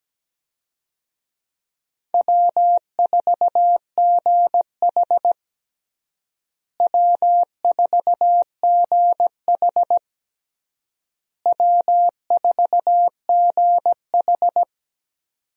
Morse code, heard as a keyed tone, about 17 wpm, 700 Hz. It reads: W4GH W4GH W4GH
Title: USA Call Signs - 4 Characters - Repeated 3x - 17wpm